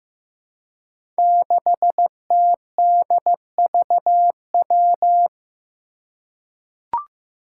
Morse code, heard as a keyed tone, about 15 wpm, 700 Hz. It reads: 6TDVW
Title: Five Challenging Letter-Number Combinations 15wpm